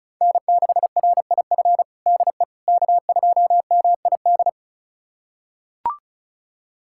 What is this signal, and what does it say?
Morse code, heard as a keyed tone, about 36 wpm, 700 Hz. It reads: N6RIF DE K2MID
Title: USA Callsign DE Callsign - 5 Characters - 35wpm